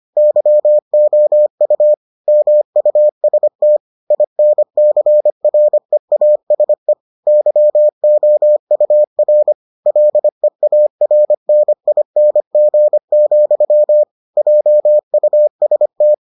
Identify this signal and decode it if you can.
Morse code, heard as a keyed tone, about 25 wpm, 600 Hz. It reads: YOU MUST INCREASE YOUR LEARNING, JUST